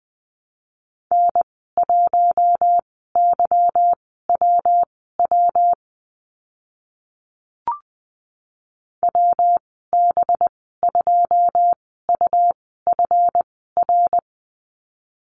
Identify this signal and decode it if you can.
Morse code, heard as a keyed tone, about 20 wpm, spaced as for 15 wpm, 700 Hz. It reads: N1YWW WB2UFR
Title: 186 Lesson 30g - Letter b - Call Signs 20wpm